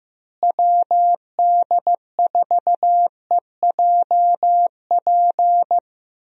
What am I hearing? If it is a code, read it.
Morse code, heard as a keyed tone, about 16 wpm, 700 Hz. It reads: WD4EJP
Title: USA Call Signs - 6 Characters - Repeated 3x - 15wpm